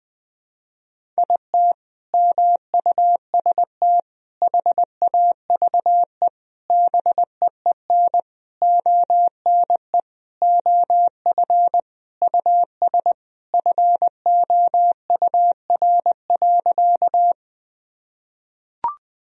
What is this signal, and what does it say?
Morse code, heard as a keyed tone, about 20 wpm, 700 Hz. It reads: IT MUST HAVE BEEN ONE OF US FOUR.
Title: Sentences from Top 1000 Words - Review Part 1 - 20wpm